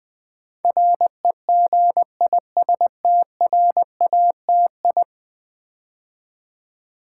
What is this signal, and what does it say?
Morse code, heard as a keyed tone, about 20 wpm, 700 Hz. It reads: REGISTRATI
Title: Build A Word - 5 Letters At A Time - 12-Letter Words - 20wpm